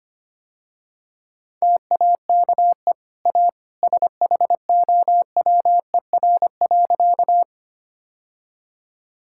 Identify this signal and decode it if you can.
Morse code, heard as a keyed tone, about 25 wpm, 700 Hz. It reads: TAKE A SHOWER.
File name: Easy Sentences with Hints - 25wpm